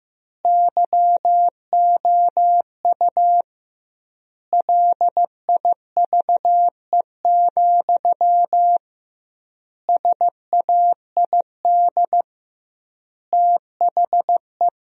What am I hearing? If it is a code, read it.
Morse code, heard as a keyed tone, about 15 wpm, 700 Hz. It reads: YOU LIVE, SAID THE
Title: 2x Word-Spacing - Sentences from Top 200 Words - 15wpm